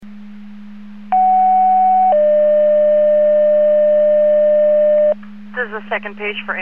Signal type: two-tone page